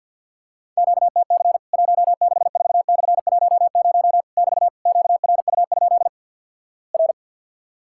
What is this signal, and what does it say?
Morse code, heard as a keyed tone, about 50 wpm, 700 Hz. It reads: QTC 164/10 <BT> QRU? R
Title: Worked All Europe DX Contest - QTC Exchange with Repeats - 50wpm